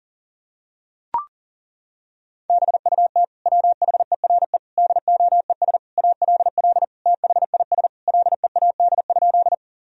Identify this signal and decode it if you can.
Morse code, heard as a keyed tone, about 40 wpm, 700 Hz. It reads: BUT WHERE DOES ALL THIS LEAD?
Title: Sentences from Top 1000 Words - Review Part 2 - 40wpm